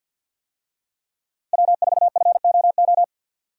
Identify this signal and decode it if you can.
Morse code, heard as a keyed tone, about 50 wpm, 700 Hz. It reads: W4FOO
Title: USA Call Signs - 5 Characters - Repeated 3x - 50wpm